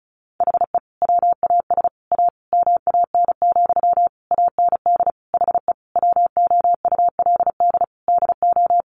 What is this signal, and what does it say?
Morse code, heard as a keyed tone, about 34 wpm, 700 Hz. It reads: HE WAS A MAN, AND HE WOULD DO
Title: Sentences from Top 1500 Words 35wpm